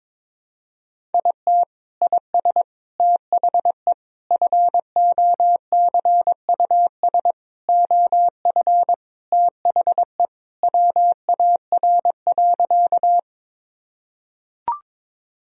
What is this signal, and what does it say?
Morse code, heard as a keyed tone, about 22 wpm, 700 Hz. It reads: IT IS THE FOCUS OF THE WAR.
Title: Sentences from Top 1000 Words - Review Part 4 - 22wpm